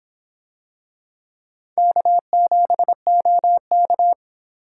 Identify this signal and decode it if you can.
Morse code, heard as a keyed tone, about 25 wpm, 700 Hz. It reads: K7OK